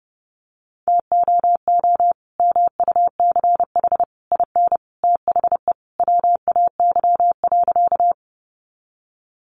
Morse code, heard as a keyed tone, about 30 wpm, 700 Hz. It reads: TOO MUCH IN THE WAY.
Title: Sentences from Top 200 Words 30wpm